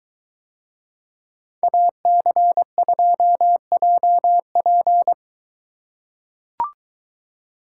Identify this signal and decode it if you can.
Morse code, heard as a keyed tone, about 22 wpm, 700 Hz. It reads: AC2JP